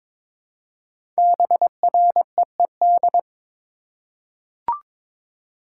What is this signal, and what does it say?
Morse code, heard as a keyed tone, about 22 wpm, 700 Hz. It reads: BREED
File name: 5 Letter Words - First 3 Letters Given - 22wpm